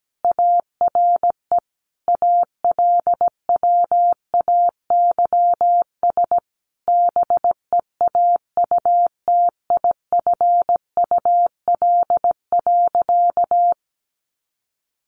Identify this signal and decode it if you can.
Morse code, heard as a keyed tone, about 17 wpm, 700 Hz. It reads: ARE ALWAYS BEAUTIFUL.
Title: Sentences from Top 900 Words 17wpm